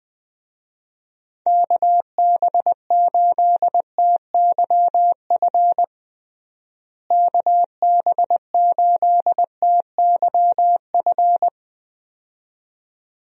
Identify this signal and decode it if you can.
Morse code, heard as a keyed tone, about 20 wpm, 700 Hz. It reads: KB8TYF KB8TYF
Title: USA Call Signs - 6 Characters - Repeated 2x - 20wpm